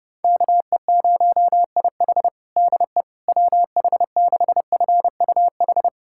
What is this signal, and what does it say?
Morse code, heard as a keyed tone, about 32 wpm, 700 Hz. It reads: KE0IH DE WH6FUH
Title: USA Callsign DE Callsign 30wpm